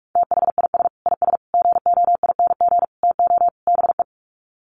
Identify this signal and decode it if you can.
Morse code, heard as a keyed tone, about 44 wpm, 700 Hz. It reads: THIS IS GOING TO BE
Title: Sets of 5 Words 45wpm